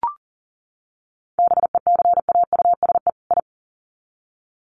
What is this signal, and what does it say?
Morse code, heard as a keyed tone, about 40 wpm, 700 Hz. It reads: BECAUSE I
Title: Sets of 2 Words 40wpm